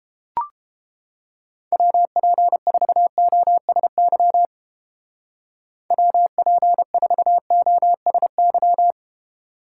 Morse code, heard as a keyed tone, about 31 wpm, 700 Hz. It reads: WP4OSY WP4OSY